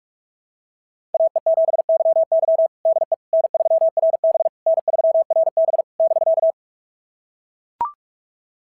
Morse code, heard as a keyed tone, about 44 wpm, 690 Hz. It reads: AE7YY DE N3RB N3RB <BK>
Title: SOTA QSO Practice 45wpm